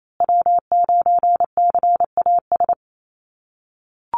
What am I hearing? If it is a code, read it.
Morse code, heard as a keyed tone, about 28 wpm, 700 Hz. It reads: W9CAS
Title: USA Call Signs Repeated 2x 28wpm